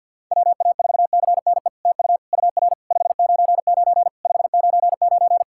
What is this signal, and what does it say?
Morse code, heard as a keyed tone, about 50 wpm, 700 Hz. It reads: WA4CNE TU UR 599 599
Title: SOTA QSO Practice 50wpm